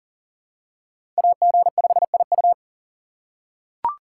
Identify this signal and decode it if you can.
Morse code, heard as a keyed tone, about 40 wpm, 700 Hz. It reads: AG5IU
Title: USA Call Signs - 5 Characters - 4x Character Spacing - 40wpm